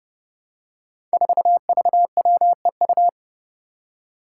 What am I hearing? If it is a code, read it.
Morse code, heard as a keyed tone, about 30 wpm, 700 Hz. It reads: 4VWEU
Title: Five Challenging Letter-Number Combinations 30wpm